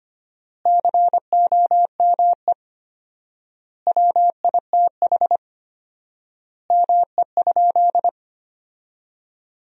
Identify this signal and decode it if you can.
Morse code, heard as a keyed tone, about 25 wpm, 700 Hz. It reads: COME WITH ME?
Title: 4x Word-Spacing - Sentences from Top 200 Words - 25wpm